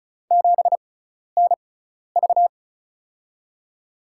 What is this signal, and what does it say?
Morse code, heard as a keyed tone, about 36 wpm, spaced as for 12 wpm, 700 Hz. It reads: 7NV
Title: USA Call Signs - 6x Character Spacing - 35wpm